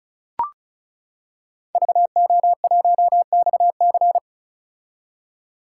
Morse code, heard as a keyed tone, about 34 wpm, 700 Hz. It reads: VO1XC
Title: K1USN SST QSO Practice - 35wpm